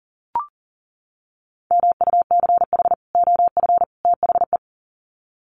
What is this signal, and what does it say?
Morse code, heard as a keyed tone, about 40 wpm, 700 Hz. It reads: MUCH OF THE
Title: Sets of 3 Words 40wpm